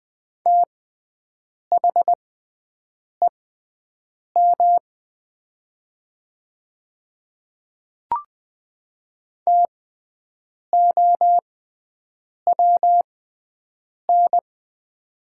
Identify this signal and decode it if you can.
Morse code, heard as a keyed tone, about 20 wpm, spaced as for 7 wpm, 700 Hz. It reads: THEM TOWN
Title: 6x Character Spacing - Top 500 Words - 20wpm